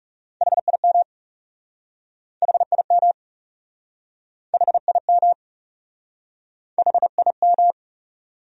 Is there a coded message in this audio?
Yes: Morse code from 0.4 to 7.7 s, heard as a keyed tone, about 40 wpm, 700 Hz. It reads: HIM HIM HIM HIM